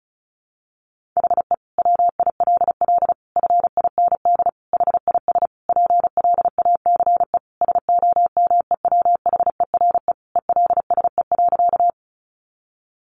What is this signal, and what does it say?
Morse code, heard as a keyed tone, about 34 wpm, 700 Hz. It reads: HE WILL FIND HIS PLACE SOMEWHERE ELSE.
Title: Sentences from Top 1400 Words 35wpm